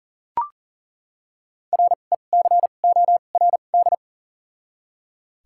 Morse code, heard as a keyed tone, about 37 wpm, spaced as for 28 wpm, 700 Hz. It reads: RECORD
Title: 2x Character Spacing - Top 500 Words - 40wpm